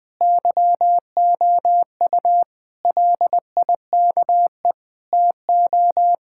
Morse code, heard as a keyed tone, about 20 wpm, 700 Hz. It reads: YOU LIKE TO
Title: Sentences with 4-Word Phrases - 20wpm